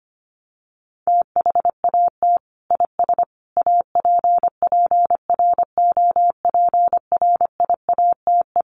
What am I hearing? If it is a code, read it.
Morse code, heard as a keyed tone, about 25 wpm, 700 Hz. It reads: THAT IS APPROPRIATE
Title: Sentences from Top 900 Words 25wpm